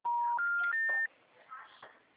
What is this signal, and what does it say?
Special information tone, with speech.